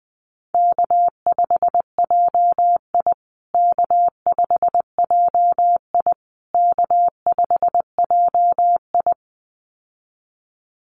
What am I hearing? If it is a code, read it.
Morse code, heard as a keyed tone, about 20 wpm, 700 Hz. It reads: K5JI K5JI K5JI
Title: USA Call Signs Repeated 3x 20wpm